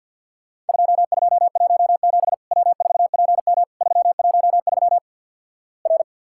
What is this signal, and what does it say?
Morse code, heard as a keyed tone, about 50 wpm, 700 Hz. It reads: 2217 W4PM 313 R
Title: Worked All Europe DX Contest - QTC Exchange - 50wpm